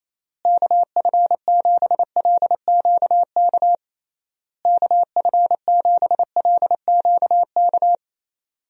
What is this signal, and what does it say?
Morse code, heard as a keyed tone, about 28 wpm, 700 Hz. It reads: KF7LQK KF7LQK
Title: USA Call Signs - 6 Characters - Repeated 3x - 28wpm